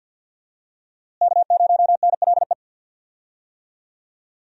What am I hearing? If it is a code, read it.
Morse code, heard as a keyed tone, about 53 wpm, 690 Hz. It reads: K0NLE
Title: USA Call Signs - 5 Characters - Repeated 3x - 50wpm